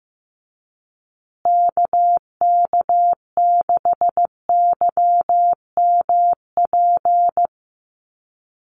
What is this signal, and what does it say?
Morse code, heard as a keyed tone, about 15 wpm, 700 Hz. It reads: KK6YMP
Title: USA Call Signs Repeated 3x 15wpm